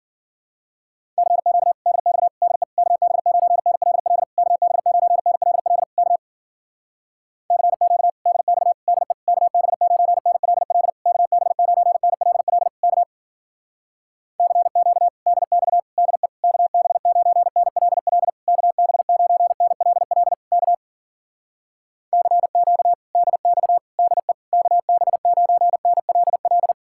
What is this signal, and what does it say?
Morse code, heard as a keyed tone, about 52 wpm, 700 Hz. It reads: CQ DX DE KB9NLL KB9NLL K CQ DX DE KB9NLL KB9NLL K CQ DX DE KB9NLL KB9NLL K CQ DX DE KB9NLL